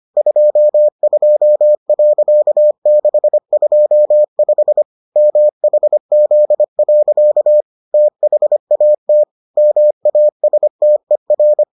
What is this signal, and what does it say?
Morse code, heard as a keyed tone, about 25 wpm, 600 Hz. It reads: 22.625 MHZ. THAT MASTER